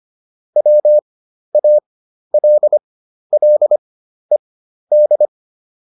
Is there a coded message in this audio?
Yes: Morse code from 0.6 to 5.3 s, heard as a keyed tone, about 25 wpm, spaced as for 12 wpm, 600 Hz. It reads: WALLED